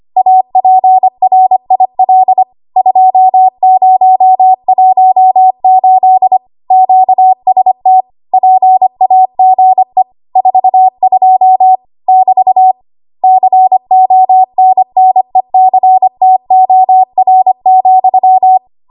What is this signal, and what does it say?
Morse code, heard as a keyed tone, about 25 wpm, 750 Hz. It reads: APRIL 2018 QST PAGE 42 <BT> CONNECTOR,
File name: Archive / 25 WPM / 25WPM